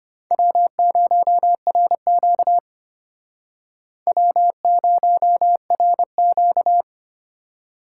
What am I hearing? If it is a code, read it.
Morse code, heard as a keyed tone, about 35 wpm, 700 Hz. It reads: W0RQ W0RQ